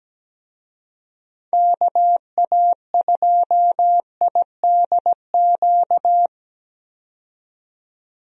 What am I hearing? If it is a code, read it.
Morse code, heard as a keyed tone, about 17 wpm, 700 Hz. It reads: KA2IDQ
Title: USA Call Signs - 6 Characters - 17wpm